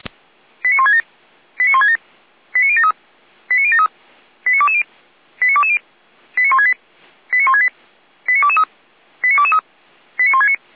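Five-tone selective call (ZVEI), heard as quick runs of five tones, tones of about 70 ms. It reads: 89167 89167 89073 89073 89220 89220 89168 89168 89223 89223 89169
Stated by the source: ZVEI five-tone selective call received on SDR, code 89060.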